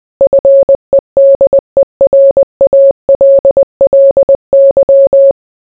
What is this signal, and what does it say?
Morse code, heard as a keyed tone, about 20 wpm, 550 Hz. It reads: FEDERALLY